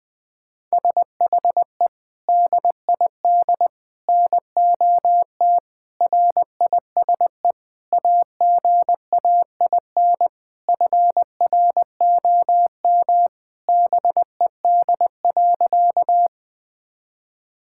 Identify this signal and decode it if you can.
Morse code, heard as a keyed tone, about 20 wpm, 700 Hz. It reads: SHE DID NOT RISE AGAIN FROM BED.
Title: Sentences from Top 1000 Words - Review Part 6 - 20wpm